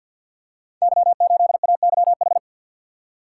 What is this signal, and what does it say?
Morse code, heard as a keyed tone, about 53 wpm, 700 Hz. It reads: Y8AYH
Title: Code Groups - 5 Letter:Numbers - 50wpm